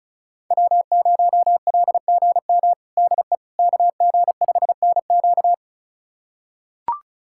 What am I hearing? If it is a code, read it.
Morse code, heard as a keyed tone, about 34 wpm, 700 Hz. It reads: W0LGM DE KG5NQ
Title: USA Callsign DE Callsign - 5 Characters - 35wpm